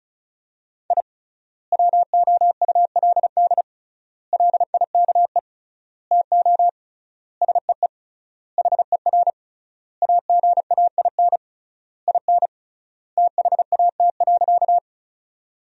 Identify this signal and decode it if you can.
Morse code, heard as a keyed tone, about 36 wpm, 700 Hz. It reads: I WOULD LIKE TO SEE HER AGAIN IN THAT.
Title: 3x Word-Spacing - Sentences from Top 200 Words - 35wpm